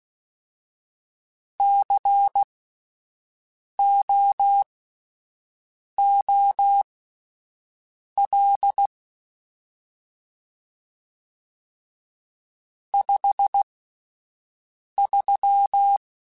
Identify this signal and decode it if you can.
Morse code, heard as a keyed tone, about 16 wpm, spaced as for 5 wpm, 800 Hz. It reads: COOL 53
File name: Spaced Characters / Spaced Characters-mp3s / Weather 5 1 sp @10